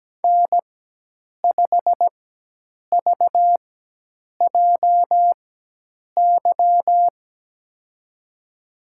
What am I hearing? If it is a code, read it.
Morse code, heard as a keyed tone, about 17 wpm, spaced as for 8 wpm, 700 Hz. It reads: N5VJY
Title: USA Call Signs - 5 Characters - 4x Character Spacing - 17wpm